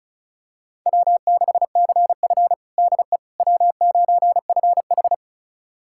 Morse code, heard as a keyed tone, about 34 wpm, 700 Hz. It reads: W6CF DE W9FH
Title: USA Callsign DE Callsign - 4 Characters - 35wpm